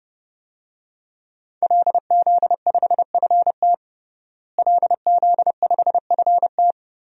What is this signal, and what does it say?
Morse code, heard as a keyed tone, about 30 wpm, 700 Hz. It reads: LZ5FT LZ5FT
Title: DXCC-100 Call Signs - 5 Characters - Repeated 2x - 30wpm